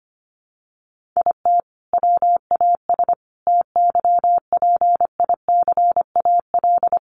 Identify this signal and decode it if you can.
Morse code, heard as a keyed tone, about 25 wpm, 700 Hz. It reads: IT WAS TYPICAL
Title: Sentences from Top 1100 Words 25wpm